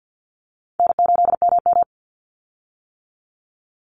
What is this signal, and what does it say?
Morse code, heard as a keyed tone, about 50 wpm, 700 Hz. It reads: N8MM
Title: USA Call Signs 50wpm